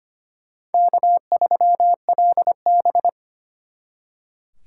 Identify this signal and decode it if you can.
Morse code, heard as a keyed tone, about 33 wpm, 700 Hz. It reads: K3LB